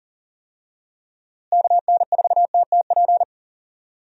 Morse code, heard as a keyed tone, about 40 wpm, 700 Hz. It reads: KN4TTP
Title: USA Call Signs - 6 Characters - Repeated 2x - 40wpm